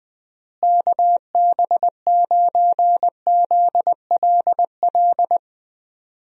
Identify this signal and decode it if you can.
Morse code, heard as a keyed tone, about 20 wpm, 700 Hz. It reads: KB9ZLL
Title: USA Call Signs - 6 Characters - Repeated 3x - 20wpm